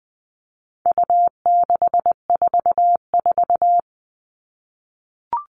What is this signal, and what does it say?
Morse code, heard as a keyed tone, about 20 wpm, 700 Hz. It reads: U644
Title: All Cut Numbers - 4-Digit 20wpm